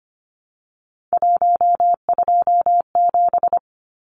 Morse code, heard as a keyed tone, about 25 wpm, 700 Hz. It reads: 127
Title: Numbers - 3-Digit 25wpm